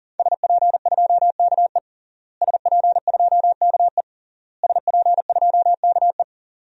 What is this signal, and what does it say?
Morse code, heard as a keyed tone, about 40 wpm, 700 Hz. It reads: SP2KE SP2KE SP2KE
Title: DXCC-100 Call Signs - 5 Characters - Repeated 3x - 40wpm